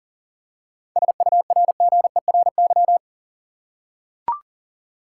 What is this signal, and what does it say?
Morse code, heard as a keyed tone, about 40 wpm, 700 Hz. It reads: SURGERY
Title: Seven Letter Words - 40wpm